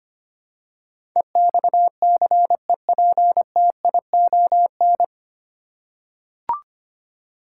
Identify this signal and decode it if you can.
Morse code, heard as a keyed tone, about 25 wpm, 700 Hz. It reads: EXCEPTION
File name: practice / 25wpm / 9 Letter Words - First 3 Letters Given - 25wpm